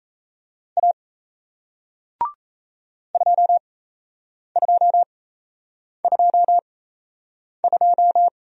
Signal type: Morse code

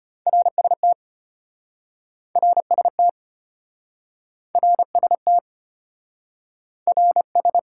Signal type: Morse code

